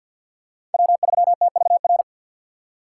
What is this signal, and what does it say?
Morse code, heard as a keyed tone, about 53 wpm, 700 Hz. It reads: W3TVR
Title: USA Call Signs - 5 Characters - 6x Character Spacing - 50wpm